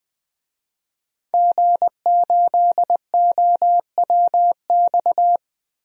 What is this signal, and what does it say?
Morse code, heard as a keyed tone, about 20 wpm, 700 Hz. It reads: G8OWX
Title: DXCC-100 Call Signs - 5 Characters - Repeated 3x - 20wpm